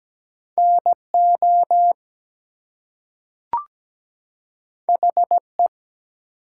Morse code, heard as a keyed tone, about 17 wpm, 700 Hz. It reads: NO HE
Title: Two Letter Words - 17wpm